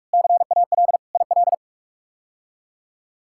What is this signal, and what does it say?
Morse code, heard as a keyed tone, about 46 wpm, 700 Hz. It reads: CAL IL
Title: K1USN SST QSO Practice - 45wpm